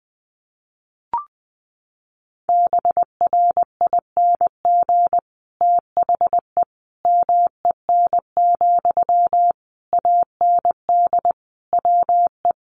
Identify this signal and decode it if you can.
Morse code, heard as a keyed tone, about 20 wpm, 700 Hz. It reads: BRING THE MEN, AND WE
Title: Sentences from Top 1400 Words 20wpm